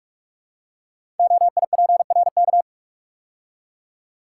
Morse code, heard as a keyed tone, about 44 wpm, 690 Hz. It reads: OIPRK
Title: Code Groups - 5 Letters - 45wpm